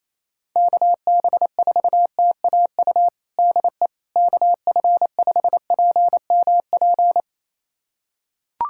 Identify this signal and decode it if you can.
Morse code, heard as a keyed tone, about 28 wpm, 700 Hz. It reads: KB4TAU DE KF5PMP
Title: USA Callsign DE Callsign - 6 Characters - 28wpm